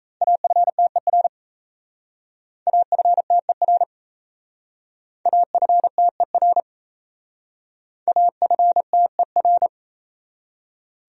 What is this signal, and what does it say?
Morse code, heard as a keyed tone, about 34 wpm, 700 Hz. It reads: AFTER AFTER AFTER AFTER
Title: Speed-Racing Top 200 Words 28wpm